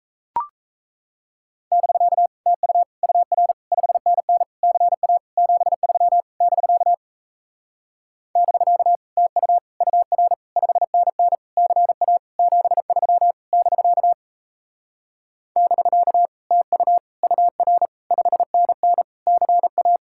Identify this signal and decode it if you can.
Morse code, heard as a keyed tone, about 38 wpm, 700 Hz. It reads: <BK> TU UR 5NN CA 73 <BK> <BK> TU UR 5NN CA 73 <BK> <BK> TU UR 5NN CA